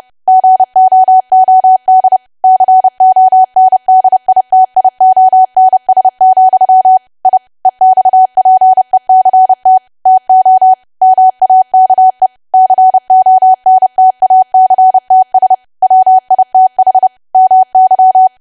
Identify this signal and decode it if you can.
Morse code, heard as a keyed tone, about 30 wpm, 750 Hz. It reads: GOOD CONDITIONS, I EXPECT TO MAKE CONTACTS WITH MY
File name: cwa / W1AW-Level3 / 30WPM